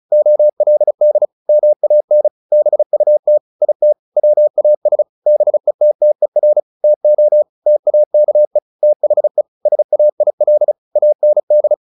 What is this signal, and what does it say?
Morse code, heard as a keyed tone, about 34 wpm, 600 Hz. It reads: OLD MAN BUT IT WAS BETTER TO TAKE THE SAIL AND
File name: the old man and the sea / the old man and the sea 1 35wpm